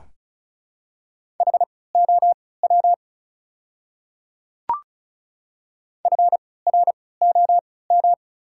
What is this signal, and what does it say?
Morse code, heard as a keyed tone, about 34 wpm, spaced as for 20 wpm, 690 Hz. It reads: HOW FROM
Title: 3x Character Spacing - Top 100 Words - 35wpm